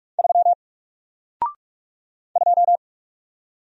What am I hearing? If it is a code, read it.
Morse code, heard as a keyed tone, about 44 wpm, 700 Hz. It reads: 3 2
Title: Single Character (Letter:Number:Punctuation:Prosign) Alt 45wpm